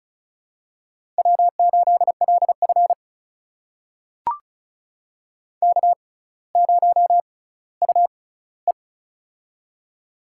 Morse code, heard as a keyed tone, about 36 wpm, spaced as for 18 wpm, 700 Hz. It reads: W8LF K0UE
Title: USA Call Signs - 4 Characters - 6x Character Spacing - 35wpm